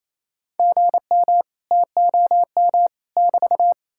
Morse code, heard as a keyed tone, about 38 wpm, 700 Hz. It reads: GM TOM <BT>